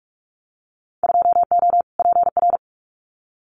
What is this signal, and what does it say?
Morse code, heard as a keyed tone, about 44 wpm, 700 Hz. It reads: 2O PR
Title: Winter Field Day Exchanges Encore 45wpm